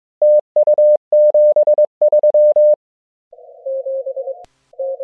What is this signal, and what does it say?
Morse code, heard as a keyed tone, about 22 wpm, 600 Hz. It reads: TU 73